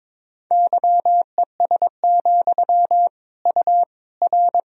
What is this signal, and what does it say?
Morse code, heard as a keyed tone, about 22 wpm, 700 Hz. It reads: YES, U R
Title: Sentences from Top 500 Words and Abbreviations 22wpm